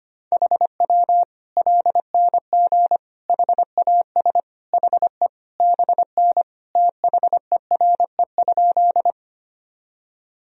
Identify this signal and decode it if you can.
Morse code, heard as a keyed tone, about 25 wpm, 700 Hz. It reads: HW LNG HAS HE BN THERE?
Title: Sentences from Top 100 Words and Abbreviations 25wpm